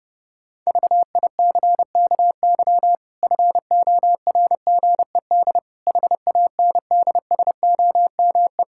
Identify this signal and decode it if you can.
Morse code, heard as a keyed tone, about 30 wpm, 700 Hz. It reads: VICKY FORGED HANDSOME
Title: Pangram Challenge v2 - 30wpm